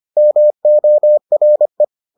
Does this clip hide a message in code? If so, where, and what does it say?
Yes: Morse code from 0.2 to 1.8 s, heard as a keyed tone, about 25 wpm, 600 Hz. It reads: MORE